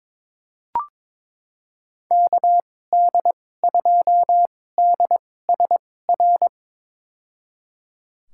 Morse code, heard as a keyed tone, about 22 wpm, spaced as for 16 wpm, 700 Hz. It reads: KD2DSR
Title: USA Call Signs - 6 Characters - 2x Character Spacing - 22wpm